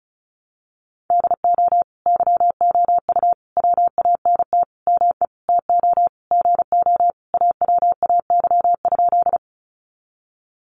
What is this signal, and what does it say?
Morse code, heard as a keyed tone, about 34 wpm, 700 Hz. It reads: DO YOU WANT ME TO GO AWAY?
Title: Sentences from Top 600 Words 35wpm